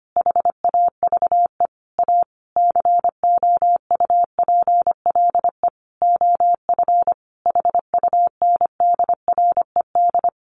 Morse code, heard as a keyed tone, about 25 wpm, 700 Hz. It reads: HAVE A COUPLE OF HUNDRED
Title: Sentences from Top 1100 Words 25wpm